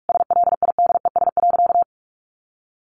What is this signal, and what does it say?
Morse code, heard as a keyed tone, about 44 wpm, 700 Hz. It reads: SLIDES.
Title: Easy Phrases 45wpm